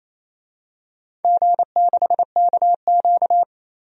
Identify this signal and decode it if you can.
Morse code, heard as a keyed tone, about 28 wpm, 700 Hz. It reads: G6KQ
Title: DXCC-100 Call Signs - 4 Characters - 28wpm